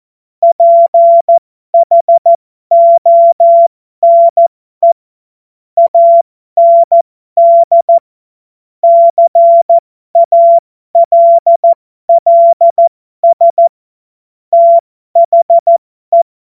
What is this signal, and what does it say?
Morse code, heard as a keyed tone, about 14 wpm, spaced as for 12 wpm, 680 Hz. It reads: PHONE AND CALLS THE
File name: Morsekurs / Witze